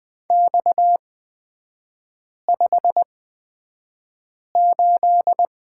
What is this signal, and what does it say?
Morse code, heard as a keyed tone, about 20 wpm, spaced as for 5 wpm, 700 Hz. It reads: X58
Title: 236 Lesson 38e - Letter q - Three Characters 20wpm